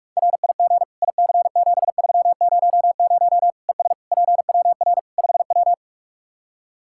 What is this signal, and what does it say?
Morse code, heard as a keyed tone, about 44 wpm, 700 Hz. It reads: RIG IC7300 ES PWR 5W
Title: Practice QSO - Exchange 2 - Info - 45wpm